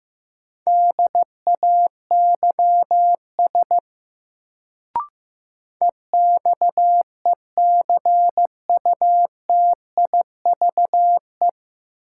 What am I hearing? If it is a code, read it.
Morse code, heard as a keyed tone, about 16 wpm, 700 Hz. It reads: DAYS EXECUTIVE
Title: Sets of 5 Words Encore - New Word Drill - 15wpm